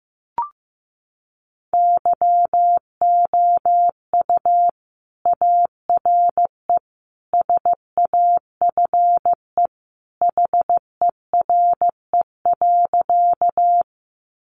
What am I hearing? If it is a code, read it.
Morse code, heard as a keyed tone, about 15 wpm, 700 Hz. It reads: YOU ARE SAFE HERE.
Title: Sentences from Top 900 Words 15wpm